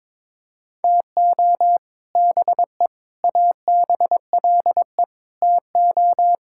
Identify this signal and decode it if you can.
Morse code, heard as a keyed tone, about 22 wpm, 700 Hz. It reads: TO BE ABLE TO
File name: practice / 22wpm / Sentences with 3-Word Phrases - 22wpm